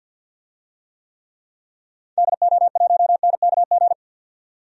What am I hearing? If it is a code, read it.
Morse code, heard as a keyed tone, about 52 wpm, 700 Hz. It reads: DO1NKG
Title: DXCC-100 Call Signs - 6 Characters - 50wpm